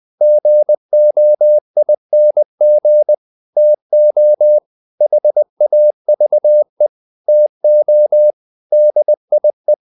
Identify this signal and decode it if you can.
Morse code, heard as a keyed tone, about 20 wpm, 600 Hz. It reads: GOING TO HAVE TO DIE
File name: the old man and the sea part 2 20wpm